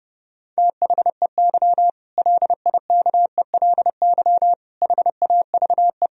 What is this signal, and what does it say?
Morse code, heard as a keyed tone, about 30 wpm, 700 Hz. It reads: THEY LIKELY HAVE
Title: Sentences from Top 1000 Words - Review Part 4 - 30wpm